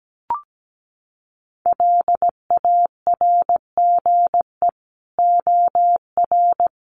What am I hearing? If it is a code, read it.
Morse code, heard as a keyed tone, about 17 wpm, 700 Hz. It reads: LARGE OR
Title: Sentences from Top 200 Words 17wpm